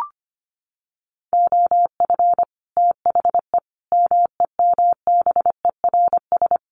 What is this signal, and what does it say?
Morse code, heard as a keyed tone, about 25 wpm, 700 Hz. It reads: OF THE MEMBERS
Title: Sets of 5 Words Encore2 25wpm